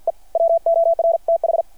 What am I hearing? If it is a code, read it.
Morse code, heard as a keyed tone, about 53 wpm, 660 Hz. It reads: E WORTH
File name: archived / CRSnet